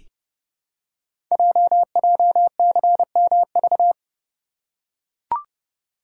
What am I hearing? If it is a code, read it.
Morse code, heard as a keyed tone, about 30 wpm, 690 Hz. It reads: JJCMV
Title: Code Groups - 5 Letters - 30wpm